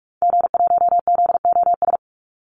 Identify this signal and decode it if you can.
Morse code, heard as a keyed tone, about 44 wpm, 700 Hz. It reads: G1ZOS
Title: Call Signs - 45wpm